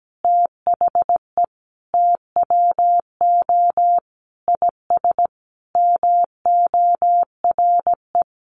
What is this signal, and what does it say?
Morse code, heard as a keyed tone, about 17 wpm, 700 Hz. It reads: THE TWO IS MORE
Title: Sentences from Top 1200 Words 17wpm